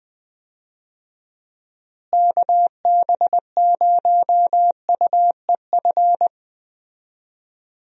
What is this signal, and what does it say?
Morse code, heard as a keyed tone, about 20 wpm, 700 Hz. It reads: KB0UEF
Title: USA Call Signs - 6 Characters - Repeated 2x - 20wpm